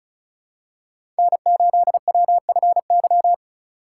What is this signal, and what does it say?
Morse code, heard as a keyed tone, about 34 wpm, 700 Hz. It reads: N8WFY
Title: USA Call Signs - 5 Characters - 2x Character Spacing - 35wpm